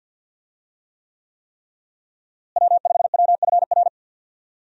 Morse code, heard as a keyed tone, about 51 wpm, 700 Hz. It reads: W5WFR
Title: USA Call Signs - 5 Characters - Repeated 3x - 50wpm